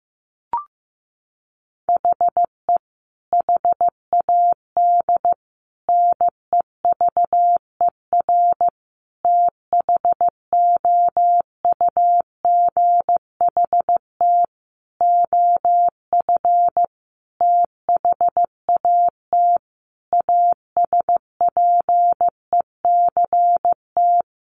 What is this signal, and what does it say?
Morse code, heard as a keyed tone, about 15 wpm, 700 Hz. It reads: HE HAD NEVER THOUGHT OF THAT ASPECT
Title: Sentences from Top 1300 Words 15wpm